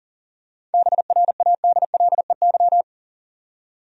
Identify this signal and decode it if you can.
Morse code, heard as a keyed tone, about 40 wpm, 700 Hz. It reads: BRADLEY
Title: USA Common Names - 40wpm